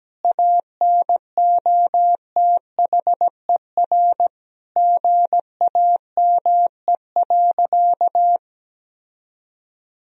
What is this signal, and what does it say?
Morse code, heard as a keyed tone, about 17 wpm, 700 Hz. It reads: ANOTHER GAME.
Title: Sentences from Top 1000 Words - Review Part 3 - 17wpm